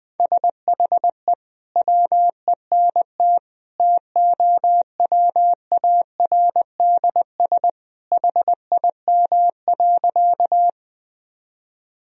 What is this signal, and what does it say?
Morse code, heard as a keyed tone, about 20 wpm, 700 Hz. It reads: SHE WENT TOWARDS HIM.
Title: Sentences from Top 1000 Words - Review Part 2 - 20wpm